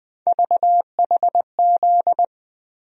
Morse code, heard as a keyed tone, about 21 wpm, 700 Hz. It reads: VHZ
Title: USA Call Sign Postfixes - 3 Letters 20wpm